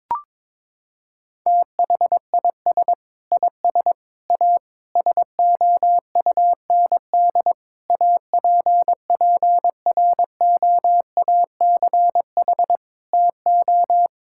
Morse code, heard as a keyed tone, about 22 wpm, 700 Hz. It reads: THIS IS A SOUND APPROACH TO
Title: Sentences from Top 1000 Words - Review Part 6 - 22wpm